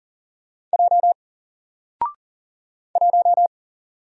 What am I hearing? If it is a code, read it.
Morse code, heard as a keyed tone, about 40 wpm, 700 Hz. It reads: J 1
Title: Single Letter-Number Alt 40wpm